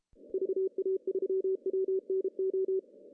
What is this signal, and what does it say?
Morse code, heard as a keyed tone, about 30 wpm, 380 Hz. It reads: VA3WNO
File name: CW / cqww-cw-k5tr